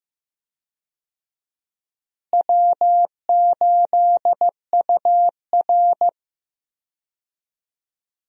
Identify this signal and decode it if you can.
Morse code, heard as a keyed tone, about 15 wpm, 700 Hz. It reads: W8UR
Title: USA Call Signs - 4 Characters - Repeated 2x - 15wpm